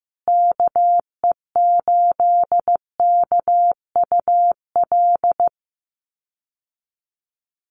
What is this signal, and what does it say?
Morse code, heard as a keyed tone, about 15 wpm, 700 Hz. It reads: KE8KUL
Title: USA Call Signs Repeated 3x 15wpm